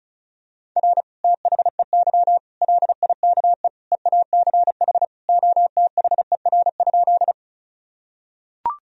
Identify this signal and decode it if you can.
Morse code, heard as a keyed tone, about 34 wpm, 690 Hz. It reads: R THEY LIKE EACH OTHER?
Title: Sentences from Top 200 Words and Abbreviations 35wpm